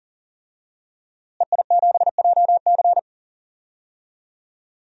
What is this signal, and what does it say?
Morse code, heard as a keyed tone, about 40 wpm, 700 Hz. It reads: EI7JC
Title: DXCC-100 Call Signs - 5 Characters - Repeated 2x - 40wpm